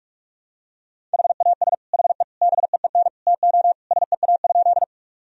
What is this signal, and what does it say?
Morse code, heard as a keyed tone, about 44 wpm, 700 Hz. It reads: HAS HE BEEN TO SEA?
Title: Sentences from Top 1000 Words - Review Part 1 - 45wpm